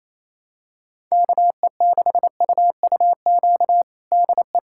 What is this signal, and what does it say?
Morse code, heard as a keyed tone, about 28 wpm, 700 Hz. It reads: KE6UUQ DE
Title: USA Callsign DE Callsign 28wpm